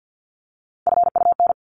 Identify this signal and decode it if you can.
Morse code, heard as a keyed tone, about 50 wpm, 700 Hz. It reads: FUN
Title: Top 501-750 Words 50wpm